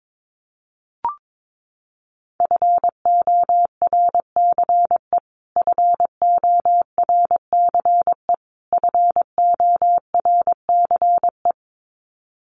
Morse code, heard as a keyed tone, about 22 wpm, 700 Hz. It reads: FORCE FORCE FORCE
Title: Top 500 Words Repeated 22wpm